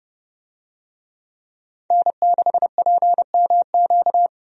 Morse code, heard as a keyed tone, about 30 wpm, 700 Hz. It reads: N6PMQ
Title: USA Call Signs - 5 Characters - 30wpm